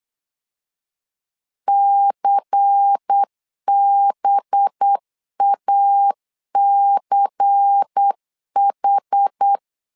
Morse code, heard as a keyed tone, about 8 wpm, 800 Hz. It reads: CBACH
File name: LEZIONI / 5 LBVC / 5 LBVC 8wpm 2935